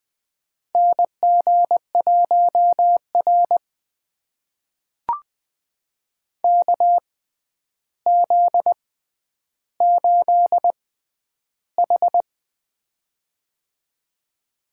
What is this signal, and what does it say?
Morse code, heard as a keyed tone, about 20 wpm, spaced as for 10 wpm, 700 Hz. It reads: NG1R KZ8H
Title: USA Call Signs - 4 Characters - 6x Character Spacing - 20wpm